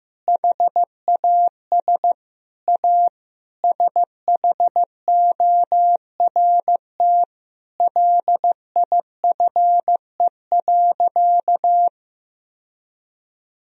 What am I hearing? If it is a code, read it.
Morse code, heard as a keyed tone, about 15 wpm, 700 Hz. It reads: HAS A SHORT LIFE.
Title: Sentences from Top 1000 Words - Review Part 2 - 15wpm